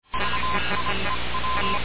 Morse code, heard as a keyed tone, about 26 wpm, spaced as for 22 wpm, 1030 Hz, with noise. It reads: VIEW
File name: radiocap / REC z